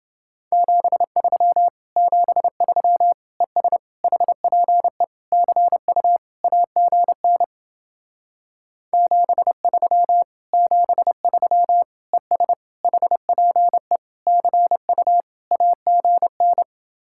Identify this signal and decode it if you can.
Morse code, heard as a keyed tone, about 29 wpm, 700 Hz. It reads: 73 73 ES HPE CU AGN 73 73 ES HPE CU AGN